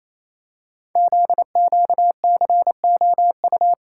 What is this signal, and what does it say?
Morse code, heard as a keyed tone, about 29 wpm, 700 Hz. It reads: ZQCOU
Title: Code Groups - 5 Letters - 28wpm